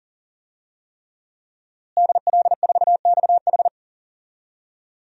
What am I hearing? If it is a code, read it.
Morse code, heard as a keyed tone, about 39 wpm, 700 Hz. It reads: DL4XH
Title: DXCC-100 Call Signs - 5 Characters - Repeated 3x - 40wpm